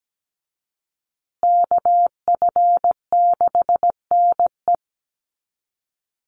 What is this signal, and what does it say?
Morse code, heard as a keyed tone, about 17 wpm, 700 Hz. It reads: KF6NE
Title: USA Call Signs Repeated 3x 17wpm